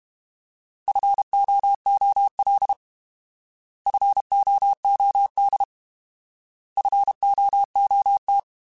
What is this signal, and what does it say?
Morse code, heard as a keyed tone, about 32 wpm, 800 Hz. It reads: FOOL FOOD FOOT
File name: Alphabetical / Alphabetical mp3s / Fs @30